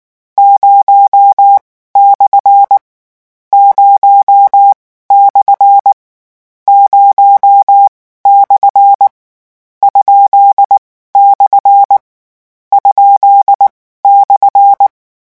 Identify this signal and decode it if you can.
Morse code, heard as a keyed tone, about 19 wpm, spaced as for 14 wpm, 800 Hz. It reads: 0/ 0/ 0/ ?/ ?/
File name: morse mp3 / lecon13-vitesse 18